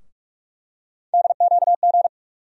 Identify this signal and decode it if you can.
Morse code, heard as a keyed tone, about 42 wpm, 700 Hz. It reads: DQG
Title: USA Call Sign Postfixes - 3 Letters 45wpm